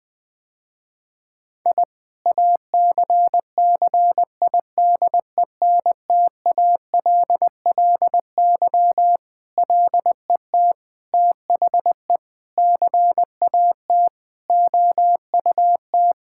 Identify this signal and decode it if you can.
Morse code, heard as a keyed tone, about 21 wpm, 700 Hz. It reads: I ACCIDENTALLY LET THE CAT OUT
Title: Sentences with 7-Word Phrases - Spoken Beginning - 20wpm